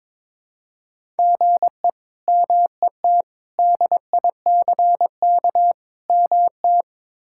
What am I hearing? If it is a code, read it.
Morse code, heard as a keyed tone, about 22 wpm, 700 Hz. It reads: GE MET DICK MT
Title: K1USN SST QSO Practice - 22wpm